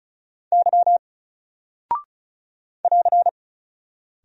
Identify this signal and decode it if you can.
Morse code, heard as a keyed tone, about 36 wpm, 700 Hz. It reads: Y <AR>
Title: Single Character (Letter:Number:Punctuation:Prosign) 35wpm